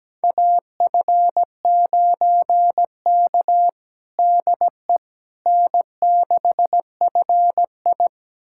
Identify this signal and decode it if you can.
Morse code, heard as a keyed tone, about 17 wpm, 700 Hz. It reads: AF9K DE N6FI
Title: USA Callsign DE Callsign - 4 Characters - 17wpm